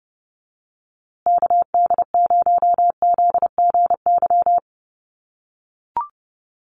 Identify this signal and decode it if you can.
Morse code, heard as a keyed tone, about 30 wpm, 700 Hz. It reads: KD0ZGY
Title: USA Call Signs Repeated 3x 30wpm